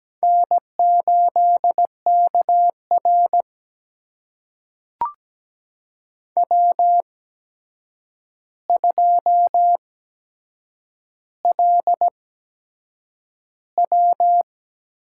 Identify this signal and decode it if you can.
Morse code, heard as a keyed tone, about 17 wpm, spaced as for 7 wpm, 700 Hz. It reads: N8KR W2LW
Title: USA Call Signs - 4 Characters - 8x Character Spacing - 17wpm